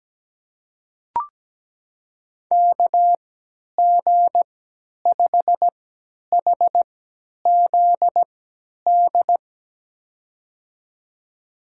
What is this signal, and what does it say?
Morse code, heard as a keyed tone, about 17 wpm, spaced as for 10 wpm, 700 Hz. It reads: KG5HZD
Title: USA Call Signs - 6 Characters - 3x Character Spacing - 17wpm